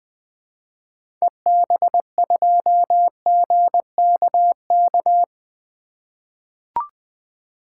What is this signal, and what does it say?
Morse code, heard as a keyed tone, about 20 wpm, 700 Hz. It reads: EB2GKK
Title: 2023 SOTA Activator Call Signs - 20wpm